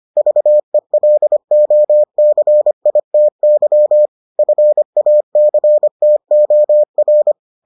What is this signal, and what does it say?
Morse code, heard as a keyed tone, about 25 wpm, 600 Hz. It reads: VELOCITY FACTOR